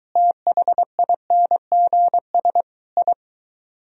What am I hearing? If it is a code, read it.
Morse code, heard as a keyed tone, about 23 wpm, 700 Hz. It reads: THINGS I